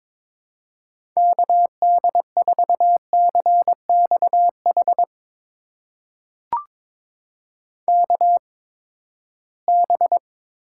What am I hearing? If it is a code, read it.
Morse code, heard as a keyed tone, about 21 wpm, spaced as for 15 wpm, 700 Hz. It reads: KD4CXH KB
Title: USA Call Signs - 6 Characters - 8x Character Spacing - 22wpm